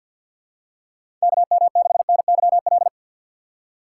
Morse code, heard as a keyed tone, about 53 wpm, 700 Hz. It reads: KM6NYL
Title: USA Call Signs - 6 Characters - 3x Character Spacing - 50wpm